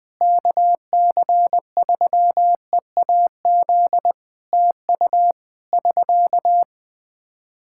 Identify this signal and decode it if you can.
Morse code, heard as a keyed tone, about 20 wpm, 700 Hz. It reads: KC3EAZ TU <SK>
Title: Practice QSO - Exchange 4 - Ending - 20wpm